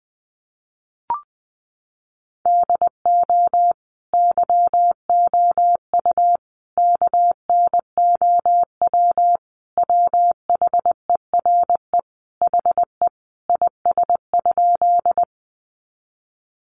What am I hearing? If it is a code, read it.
Morse code, heard as a keyed tone, about 20 wpm, 700 Hz. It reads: DO YOU KNOW WHERE HE IS?
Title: Sentences from Top 300 Words 20wpm